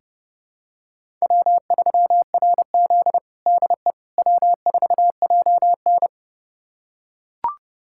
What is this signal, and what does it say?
Morse code, heard as a keyed tone, about 30 wpm, 700 Hz. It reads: W3RZ DE W4JN
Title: USA Callsign DE Callsign - 4 Characters - 30wpm